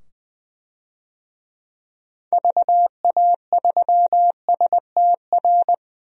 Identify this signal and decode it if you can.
Morse code, heard as a keyed tone, about 20 wpm, 700 Hz. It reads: VA3STR
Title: DXCC-100 Call Signs - 6 Characters - 20wpm